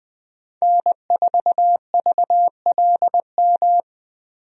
Morse code, heard as a keyed tone, about 21 wpm, 700 Hz. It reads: N4VLM
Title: USA Call Signs - 5 Characters - Repeated 2x - 20wpm